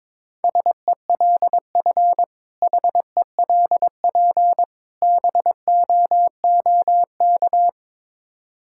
Morse code, heard as a keyed tone, about 22 wpm, 700 Hz. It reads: SELF HELP BOOK
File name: Difficult Words with Hints - 22wpm